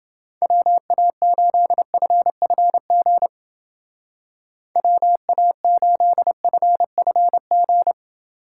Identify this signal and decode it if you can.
Morse code, heard as a keyed tone, about 28 wpm, 700 Hz. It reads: WA8FFG WA8FFG